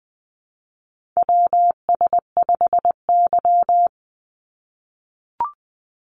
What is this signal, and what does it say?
Morse code, heard as a keyed tone, about 20 wpm, 700 Hz. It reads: WS5Y
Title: USA Call Signs 20wpm